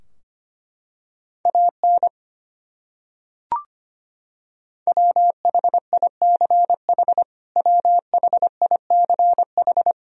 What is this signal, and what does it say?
Morse code, heard as a keyed tone, about 25 wpm, 700 Hz. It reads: AN WHICH WHICH
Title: Top 100 Words Repeated 25wpm